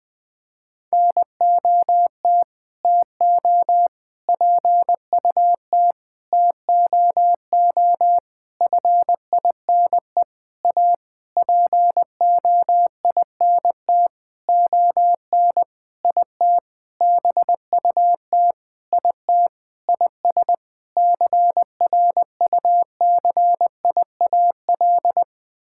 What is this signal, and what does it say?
Morse code, heard as a keyed tone, about 20 wpm, 700 Hz. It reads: NOT TO PUT TOO FINE A POINT ON IT BUT IT IS CRUCIAL
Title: Sentences with 9-Word Phrases - Spoken Beginning - 20wpm